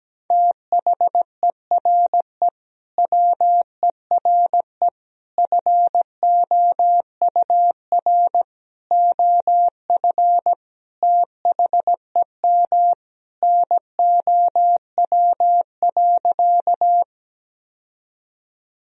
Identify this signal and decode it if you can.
Morse code, heard as a keyed tone, about 17 wpm, 700 Hz. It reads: THERE WERE FOUR OF THEM NOW.
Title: Sentences from Top 1000 Words - Review Part 1 - 17wpm